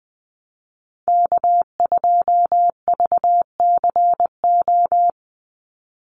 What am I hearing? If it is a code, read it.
Morse code, heard as a keyed tone, about 20 wpm, 700 Hz. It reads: K2VCO
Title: Call Signs - 20wpm